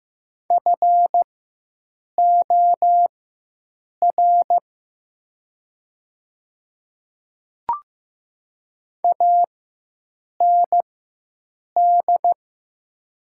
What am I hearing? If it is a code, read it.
Morse code, heard as a keyed tone, about 15 wpm, spaced as for 7 wpm, 700 Hz. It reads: FOR AND
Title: 4x Character Spacing - Top 100 Words - 15wpm